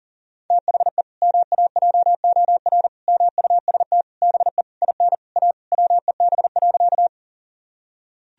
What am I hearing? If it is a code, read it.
Morse code, heard as a keyed tone, about 40 wpm, 700 Hz. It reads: THE MAJOR MUST BE IN A WEB.
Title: Sentences from Top 1000 Words - Review Part 4 - 40wpm